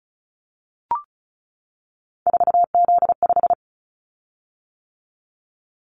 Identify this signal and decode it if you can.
Morse code, heard as a keyed tone, about 36 wpm, 700 Hz. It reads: 4Z5
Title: Callsign Prefixes for Top 50 DXCC - 35wpm